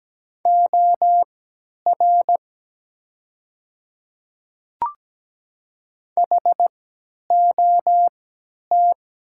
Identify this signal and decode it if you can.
Morse code, heard as a keyed tone, about 17 wpm, spaced as for 10 wpm, 700 Hz. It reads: OR HOT
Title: 3x Character Spacing - Top 100 Words - 17wpm